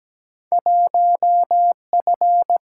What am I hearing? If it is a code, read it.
Morse code, heard as a keyed tone, about 17 wpm, 700 Hz. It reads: 1 F
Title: Single-Letter Single-Number 17wpm